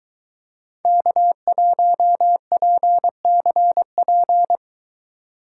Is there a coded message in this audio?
Yes: Morse code from 0.9 to 4.6 s, heard as a keyed tone, about 23 wpm, 700 Hz. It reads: K1PCP